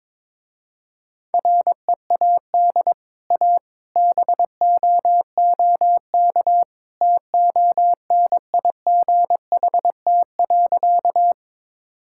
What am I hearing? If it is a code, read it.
Morse code, heard as a keyed tone, about 22 wpm, 700 Hz. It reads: READ A BOOK TONIGHT.
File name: practice / 22wpm / Easy Sentences with Hints - 22wpm